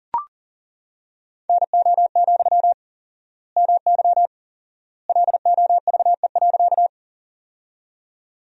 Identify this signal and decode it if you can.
Morse code, heard as a keyed tone, about 40 wpm, 700 Hz. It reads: NO, MY LOVE.
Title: 4x Word-Spacing - Sentences from Top 400 Words - 40wpm